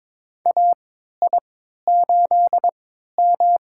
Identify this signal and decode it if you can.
Morse code, heard as a keyed tone, about 22 wpm, spaced as for 13 wpm, 700 Hz. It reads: AI8M
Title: USA Call Signs - 4 Characters - 3x Character Spacing - 22wpm